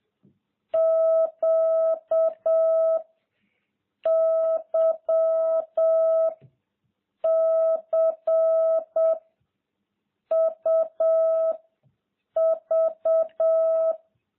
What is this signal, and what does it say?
Morse code, heard as a keyed tone, about 7 wpm, 650 Hz. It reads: Q Y C U V